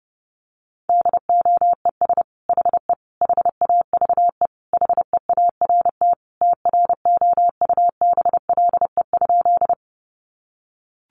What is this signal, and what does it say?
Morse code, heard as a keyed tone, about 30 wpm, 700 Hz. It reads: DOES HE HAVE HEART TROUBLE?
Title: Sentences from Top 1200 Words 30wpm